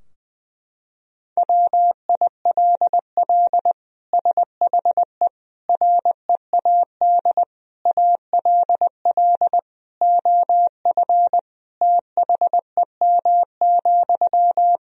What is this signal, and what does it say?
Morse code, heard as a keyed tone, about 20 wpm, 700 Hz. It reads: WILL SHE READ ALL OF THEM,
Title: Sentences from Top 1000 Words - Review Part 4 - 20wpm